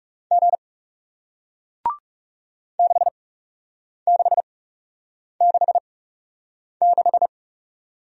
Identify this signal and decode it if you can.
Morse code, heard as a keyed tone, about 41 wpm, spaced as for 8 wpm, 700 Hz. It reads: G 6666